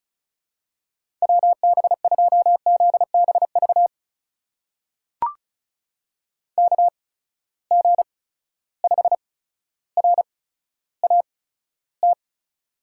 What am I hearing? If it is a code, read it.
Morse code, heard as a keyed tone, about 36 wpm, spaced as for 15 wpm, 700 Hz. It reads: WB2ZBV KG5RAT
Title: USA Call Signs - 6 Characters - 8x Character Spacing - 35wpm